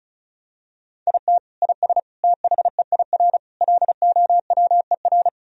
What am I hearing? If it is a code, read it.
Morse code, heard as a keyed tone, about 34 wpm, 700 Hz. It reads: IT IS THEIR LOWER
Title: Sentences from Top 1000 Words - Review Part 4 - 35wpm